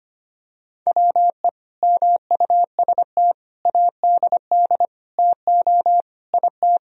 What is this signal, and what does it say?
Morse code, heard as a keyed tone, about 25 wpm, 700 Hz. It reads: WE MUST ADD TO IT
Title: Sentences from Top 1000 Words - Review Part 1 - 25wpm